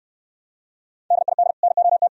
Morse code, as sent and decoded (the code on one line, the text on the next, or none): -.. .. -.. / -. --- -
DID NOT